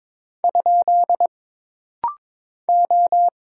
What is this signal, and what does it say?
Morse code, heard as a keyed tone, about 22 wpm, 700 Hz. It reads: ? O
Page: This rapid-fire timing is geared to push your ICR (Instant Character Recognition) skills.